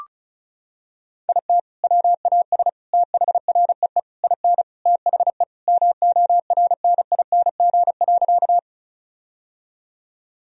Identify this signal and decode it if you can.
Morse code, heard as a keyed tone, about 34 wpm, 700 Hz. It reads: IT WAS THREE IN THE MORNING.
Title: Sentences from Top 1000 Words - Review Part 1 - 35wpm